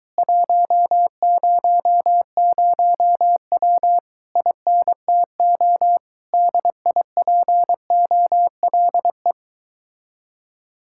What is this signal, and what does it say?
Morse code, heard as a keyed tone, about 22 wpm, 700 Hz. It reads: 100W INTO DIPOLE